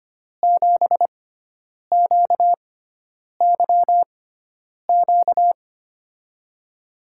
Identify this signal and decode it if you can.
Morse code, heard as a keyed tone, about 25 wpm, spaced as for 9 wpm, 700 Hz. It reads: 7QYQ
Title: USA Call Signs - 6 Characters - 6x Character Spacing - 25wpm